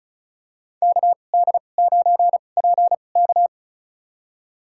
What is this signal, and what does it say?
Morse code, heard as a keyed tone, about 36 wpm, spaced as for 25 wpm, 700 Hz. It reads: KD9PK
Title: USA Call Signs - 5 Characters - 2x Character Spacing - 35wpm